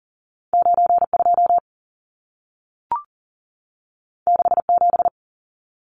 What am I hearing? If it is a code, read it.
Morse code, heard as a keyed tone, about 40 wpm, 700 Hz. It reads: 92 67
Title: Numbers - 2-Digit 40wpm